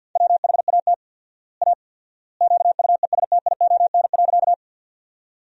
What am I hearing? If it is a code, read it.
Morse code, heard as a keyed tone, about 51 wpm, 700 Hz. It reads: WHAT A QUESTION.
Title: 4x Word-Spacing - Sentences from Top 400 Words - 50wpm